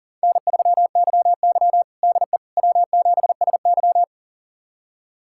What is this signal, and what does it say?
Morse code, heard as a keyed tone, about 40 wpm, 700 Hz. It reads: N3YY DE W7SY
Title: USA Callsign DE Callsign - 4 Characters - 40wpm